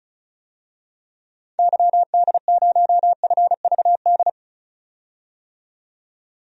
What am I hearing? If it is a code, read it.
Morse code, heard as a keyed tone, about 36 wpm, 700 Hz. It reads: YD0FVD
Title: DXCC-100 Call Signs - 6 Characters - 35wpm